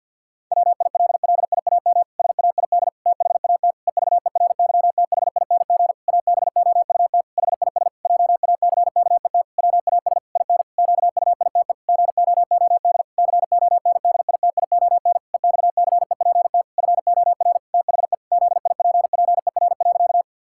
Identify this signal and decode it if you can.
Morse code, heard as a keyed tone, about 49 wpm, 700 Hz. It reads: WILLIAM SAID THAT EVERYTHING ABOUT HIS JACKET WAS IN QUITE GOOD CONDITION EXCEPT FOR THE ZIPPER.
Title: Pangram Challenge v2 - 50wpm